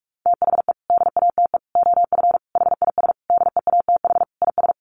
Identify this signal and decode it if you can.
Morse code, heard as a keyed tone, about 44 wpm, 700 Hz. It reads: THE DATE OF HIS DEATH IS
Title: Sentences from Top 1100 Words 45wpm